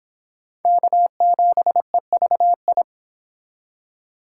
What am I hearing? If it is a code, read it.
Morse code, heard as a keyed tone, about 25 wpm, 700 Hz. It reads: K7EVI
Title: Speed Racing - USA 1x3 Call Signs 22wpm